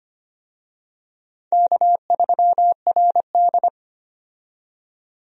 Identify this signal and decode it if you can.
Morse code, heard as a keyed tone, about 25 wpm, 700 Hz. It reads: K3RD
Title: USA Call Signs - 4 Characters - 25wpm